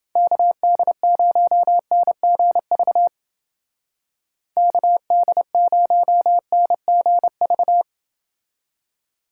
Morse code, heard as a keyed tone, about 29 wpm, 700 Hz. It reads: KD0NGV KD0NGV